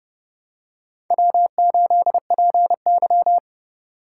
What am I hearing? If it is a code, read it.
Morse code, heard as a keyed tone, about 29 wpm, 700 Hz. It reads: W8PY